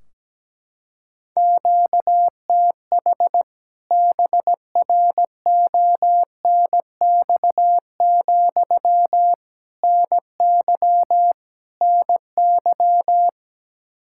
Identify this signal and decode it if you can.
Morse code, heard as a keyed tone, about 17 wpm, 700 Hz. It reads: QTH BRONX, NY NY
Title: K1USN SST QSO Practice 17wpm